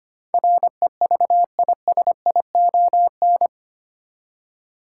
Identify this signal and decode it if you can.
Morse code, heard as a keyed tone, about 25 wpm, 700 Hz. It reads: REVISION
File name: Eight Letter Words - 25wpm